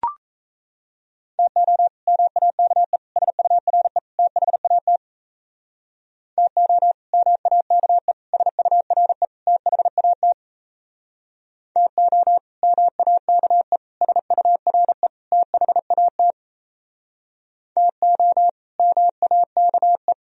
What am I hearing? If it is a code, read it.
Morse code, heard as a keyed tone, about 34 wpm, 700 Hz. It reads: TO MAKE SURE THAT TO MAKE SURE THAT TO MAKE SURE THAT TO MAKE
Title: Speed-Racing - Sets of 4 Words 28wpm